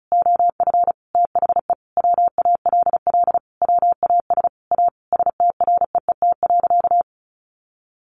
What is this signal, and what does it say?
Morse code, heard as a keyed tone, about 34 wpm, 700 Hz. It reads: OF THE WALL WAS A STREET.
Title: Sentences from Top 1100 Words 35wpm